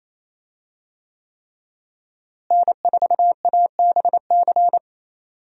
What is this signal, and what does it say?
Morse code, heard as a keyed tone, about 27 wpm, 700 Hz. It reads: N4ABC
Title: USA Call Signs - 5 Characters - Repeated 3x - 28wpm